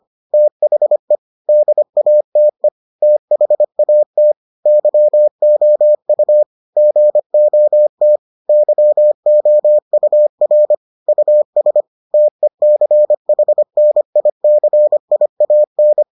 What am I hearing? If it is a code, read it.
Morse code, heard as a keyed tone, about 25 wpm, 600 Hz. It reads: THE DATE THAT YOU GOT YOUR US TECHNICIAN